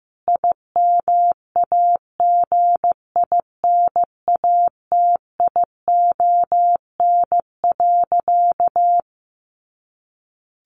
Morse code, heard as a keyed tone, about 15 wpm, 700 Hz. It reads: IMAGINATION.
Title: Sentences from Top 1300 Words 15wpm